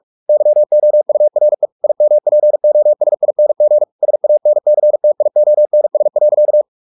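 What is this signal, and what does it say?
Morse code, heard as a keyed tone, about 44 wpm, 600 Hz. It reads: YOURE IMPOSING SANCTIONS.
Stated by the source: Morse practice at 45 wpm, the novel Never by Ken Follett in English, from CW med Gnister.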